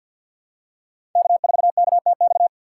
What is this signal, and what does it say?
Morse code, heard as a keyed tone, about 52 wpm, 700 Hz. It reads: K4KTX
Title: USA Call Signs - 6x Character Spacing - 50wpm